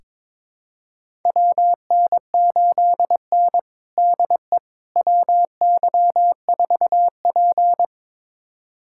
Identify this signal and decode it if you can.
Morse code, heard as a keyed tone, about 22 wpm, 700 Hz. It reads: WN8N DE WY4P
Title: USA Callsign DE Callsign - 4 Characters - 22wpm